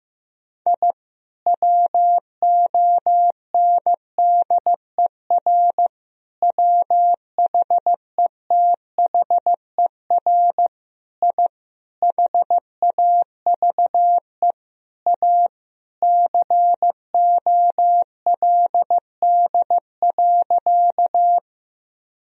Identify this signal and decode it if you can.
Morse code, heard as a keyed tone, about 15 wpm, 700 Hz. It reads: I WONDER WHETHER I HAVE A COLD.
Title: Sentences from Top 1000 Words - Review Part 3 - 15wpm